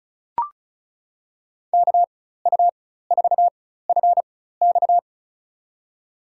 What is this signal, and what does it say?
Morse code, heard as a keyed tone, about 36 wpm, spaced as for 16 wpm, 700 Hz. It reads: KU4FX
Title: USA Call Signs - 5 Characters - 4x Character Spacing - 35wpm